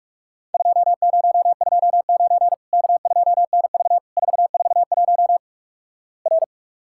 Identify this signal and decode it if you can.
Morse code, heard as a keyed tone, about 46 wpm, 700 Hz. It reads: 2029 K2NV 441 R
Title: Worked All Europe DX Contest - QTC Exchange - 45wpm